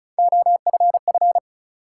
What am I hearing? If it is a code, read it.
Morse code, heard as a keyed tone, about 36 wpm, 700 Hz. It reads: OFF
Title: Sentences from Top 200 Words - New Word Drill - 35wpm